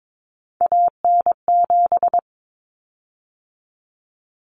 Morse code, heard as a keyed tone, about 22 wpm, 700 Hz. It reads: AN7
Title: Common Cut Numbers - 3-Digit 22wpm